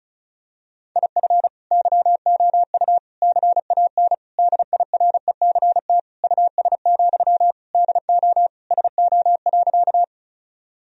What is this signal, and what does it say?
Morse code, heard as a keyed tone, about 34 wpm, 700 Hz. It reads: IF YOU CAN DIRECT US, DO SO.
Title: Sentences from Top 1000 Words - Review Part 2 - 35wpm